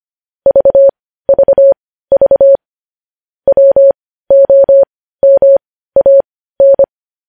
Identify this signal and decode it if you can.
Morse code, heard as a keyed tone, about 25 wpm, spaced as for 15 wpm, 550 Hz. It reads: VVV WOMAN
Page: News Headlines in Morse Code at 15 WPM